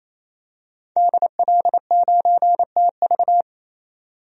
Morse code, heard as a keyed tone, about 28 wpm, 700 Hz. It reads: DL9TV
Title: DXCC-100 Call Signs - 5 Characters - 28wpm